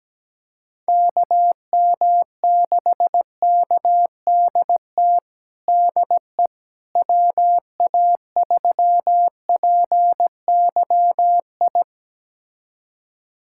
Morse code, heard as a keyed tone, about 17 wpm, 700 Hz. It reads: KM6KDT DE WA3PYI
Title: USA Callsign DE Callsign - 6 Characters - 17wpm